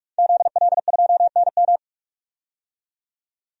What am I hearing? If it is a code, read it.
Morse code, heard as a keyed tone, about 44 wpm, 700 Hz. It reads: ZL2NM
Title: DXCC-100 Call Signs - 5 Characters - 45wpm